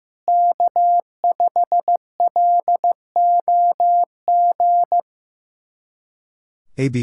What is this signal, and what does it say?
Morse code, heard as a keyed tone, about 15 wpm, 700 Hz. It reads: K5LOG
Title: USA Callsign DE Callsign - 5 Characters - 15wpm